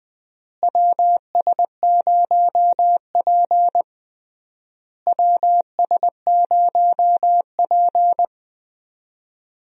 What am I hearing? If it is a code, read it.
Morse code, heard as a keyed tone, about 20 wpm, 700 Hz. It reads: WS0P WS0P
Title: USA Call Signs - 4 Characters - Repeated 3x - 20wpm